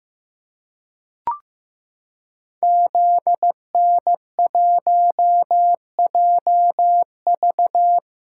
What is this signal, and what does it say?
Morse code, heard as a keyed tone, about 15 wpm, 700 Hz. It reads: ZN1JV
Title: Code Groups - 5 Letter:Numbers - 15wpm